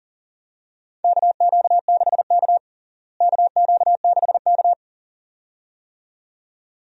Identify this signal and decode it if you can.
Morse code, heard as a keyed tone, about 40 wpm, 700 Hz. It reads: KQ6K KQ6K
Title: USA Call Signs - 4 Characters - Repeated 2x - 40wpm